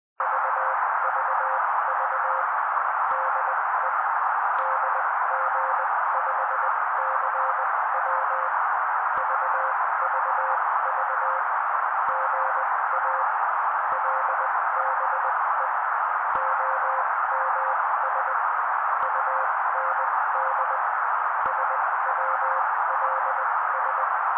Morse code, heard as a keyed tone, about 20 wpm, spaced as for 16 wpm, 530 Hz. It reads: VVV DE DG5CW VVV GA LBE OMS UND SWLS
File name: daily / 16wpm noise